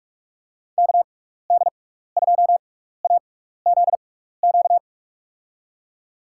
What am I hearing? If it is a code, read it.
Morse code, heard as a keyed tone, about 43 wpm, spaced as for 15 wpm, 700 Hz. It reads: KD2AZQ
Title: USA Call Signs - 6 Characters - 6x Character Spacing - 45wpm